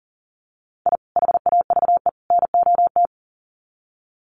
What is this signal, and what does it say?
Morse code, heard as a keyed tone, about 40 wpm, 700 Hz. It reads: I HAVE NOT
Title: Sets of 3 Words 40wpm